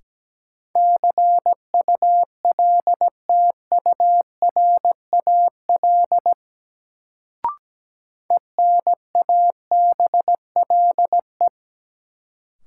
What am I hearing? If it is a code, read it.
Morse code, heard as a keyed tone, about 17 wpm, 700 Hz. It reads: CULTURAL ENABLE
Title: Sentences from Top 1500 Words - New Word Drill - 17wpm